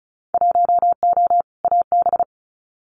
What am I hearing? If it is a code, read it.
Morse code, heard as a keyed tone, about 34 wpm, 700 Hz. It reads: 1O AB
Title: Winter Field Day Exchanges Encore 35wpm